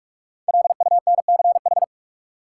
Morse code, heard as a keyed tone, about 46 wpm, 700 Hz. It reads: LUNCH
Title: Five Letter Words - 45wpm